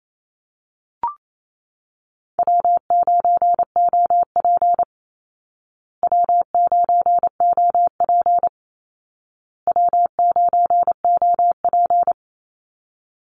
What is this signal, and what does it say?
Morse code, heard as a keyed tone, about 28 wpm, 700 Hz. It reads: W9OP W9OP W9OP
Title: Call Signs Repeated 2x - 28wpm